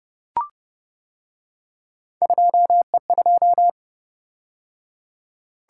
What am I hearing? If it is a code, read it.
Morse code, heard as a keyed tone, about 29 wpm, 700 Hz. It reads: 2E2